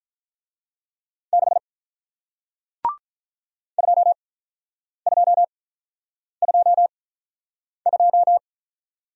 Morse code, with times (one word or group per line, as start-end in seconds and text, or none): 1.3-1.6 6
3.8-8.4 2222